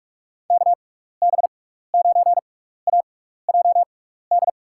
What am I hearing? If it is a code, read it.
Morse code, heard as a keyed tone, about 45 wpm, spaced as for 15 wpm, 700 Hz. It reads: KB9AJD
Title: USA Call Signs - 6 Characters - 6x Character Spacing - 45wpm